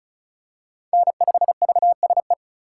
Morse code, heard as a keyed tone, about 34 wpm, 700 Hz. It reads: N5VSE
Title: USA Call Signs - 4x Character Spacing - 35wpm